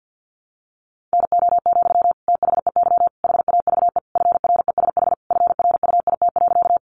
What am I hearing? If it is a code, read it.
Morse code, heard as a keyed tone, about 50 wpm, 700 Hz. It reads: NO, THEY HAVE FRESH FRUIT.
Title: Sentences from Top 1300 Words 50wpm